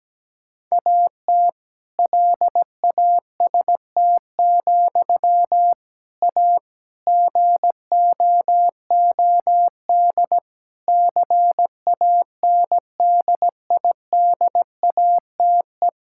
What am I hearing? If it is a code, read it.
Morse code, heard as a keyed tone, about 17 wpm, 700 Hz. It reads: AT LAST, A GOOD CANDIDATE
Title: Sentences from Top 1600 Words 17wpm